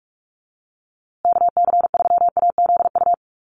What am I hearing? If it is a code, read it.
Morse code, heard as a keyed tone, about 45 wpm, 700 Hz. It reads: KC3AZU
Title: USA Call Signs Repeated 2x 45wpm